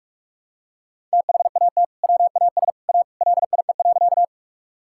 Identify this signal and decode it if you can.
Morse code, heard as a keyed tone, about 46 wpm, 700 Hz. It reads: THAT WAS A LIE.
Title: Sentences from Top 1600 Words 45wpm